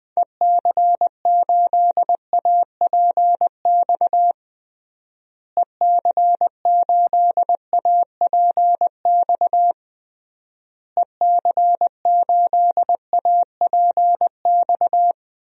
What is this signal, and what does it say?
Morse code, heard as a keyed tone, about 20 wpm, 700 Hz. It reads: EC8APX EC8APX EC8APX
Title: DXCC-100 Call Signs - 6 Characters - Repeated 3x - 20wpm